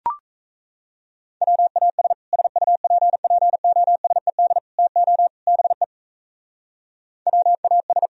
Morse code, heard as a keyed tone, about 34 wpm, 700 Hz. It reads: WAS SUPPOSED TO BE WAS
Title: Speed-Racing - Sets of 4 Words 28wpm